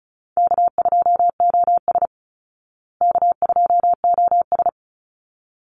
Morse code, heard as a keyed tone, about 34 wpm, 700 Hz. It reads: K2OS K2OS
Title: Easy Call Signs Repeated 2x - 35wpm